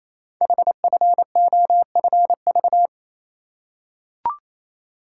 Morse code, heard as a keyed tone, about 28 wpm, 700 Hz. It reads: HFOFV
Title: Code Groups - 5 Letters - 28wpm